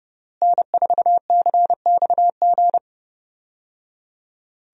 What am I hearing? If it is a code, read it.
Morse code, heard as a keyed tone, about 30 wpm, 700 Hz. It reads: N4CXG
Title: USA Call Signs - 5 Characters - Repeated 3x - 30wpm